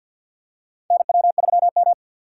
Morse code, heard as a keyed tone, about 48 wpm, 700 Hz. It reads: NW3M
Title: USA Call Signs - 4 Characters - 2x Character Spacing - 50wpm